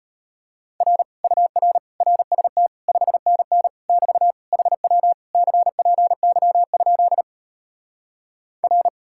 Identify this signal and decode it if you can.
Morse code, heard as a keyed tone, about 33 wpm, 700 Hz. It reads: R UR RST 5NN <BT> HW CPY? R